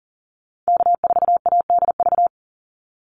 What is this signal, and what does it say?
Morse code, heard as a keyed tone, about 40 wpm, 700 Hz. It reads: K4ADV
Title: NA SOTA Activator Callsigns 40wpm